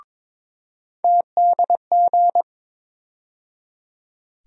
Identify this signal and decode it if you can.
Morse code, heard as a keyed tone, about 22 wpm, 700 Hz. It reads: TDG
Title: USA Call Sign Postfixes - 3 Letters 22wpm